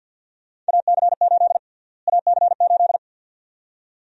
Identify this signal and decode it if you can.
Morse code, heard as a keyed tone, about 51 wpm, 700 Hz. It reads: AC8 AC8
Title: USA Call Sign Prefixes - Repeated 2x - 50wpm